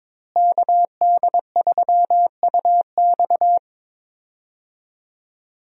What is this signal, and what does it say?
Morse code, heard as a keyed tone, about 22 wpm, 700 Hz. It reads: KD3UX
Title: USA Call Signs - 5 Characters - Repeated 3x - 22wpm